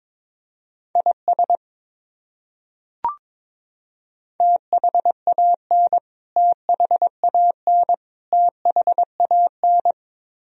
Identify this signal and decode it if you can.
Morse code, heard as a keyed tone, about 22 wpm, 700 Hz. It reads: IS THAN THAN THAN
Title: Top 100 Words Repeated 22wpm